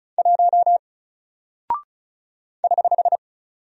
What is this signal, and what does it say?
Morse code, heard as a keyed tone, about 36 wpm, 700 Hz. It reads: 1 <HH>
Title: Single Character (Letter:Number:Punctuation:Prosign) Alt 35wpm